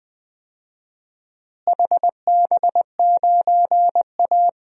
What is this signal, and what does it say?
Morse code, heard as a keyed tone, about 20 wpm, 700 Hz. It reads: HB9A
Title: DXCC-100 Call Signs - 4 Characters - Repeated 2x - 20wpm